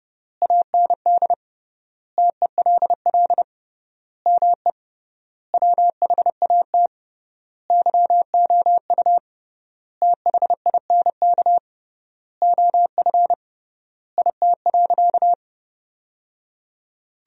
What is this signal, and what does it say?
Morse code, heard as a keyed tone, about 30 wpm, 700 Hz. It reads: AND TELL ME WHAT YOU THINK OF IT.
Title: 3x Word-Spacing - Sentences from Top 200 Words - 30wpm